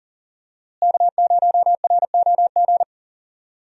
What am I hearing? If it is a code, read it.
Morse code, heard as a keyed tone, about 40 wpm, 700 Hz. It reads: K0ROG
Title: USA Call Signs - 5 Characters - Repeated 3x - 40wpm